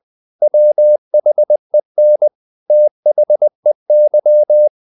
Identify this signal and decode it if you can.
Morse code, heard as a keyed tone, about 20 wpm, 600 Hz. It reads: WHEN THEY
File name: the old man and the sea / the old man and the sea part 1 20wpm